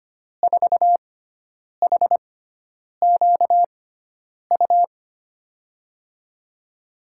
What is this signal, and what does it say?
Morse code, heard as a keyed tone, about 25 wpm, spaced as for 9 wpm, 700 Hz. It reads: 4HQU
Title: USA Call Signs - 5 Characters - 6x Character Spacing - 25wpm